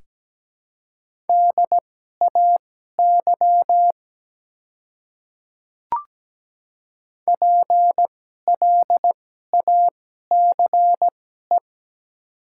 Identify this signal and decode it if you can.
Morse code, heard as a keyed tone, about 17 wpm, spaced as for 12 wpm, 700 Hz. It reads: DAY PLACE
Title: 2x Character Spacing - Top 200 Words - 17wpm